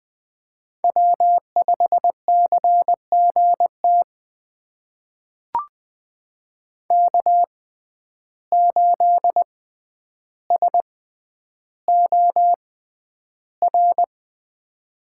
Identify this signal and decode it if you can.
Morse code, heard as a keyed tone, about 20 wpm, spaced as for 10 wpm, 700 Hz. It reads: W5CGT K8SOR
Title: USA Call Signs - 5 Characters - 6x Character Spacing - 20wpm